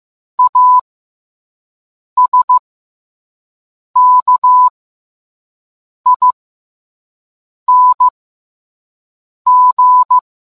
Morse code, heard as a keyed tone, about 15 wpm, spaced as for 5 wpm, 1000 Hz. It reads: ASKING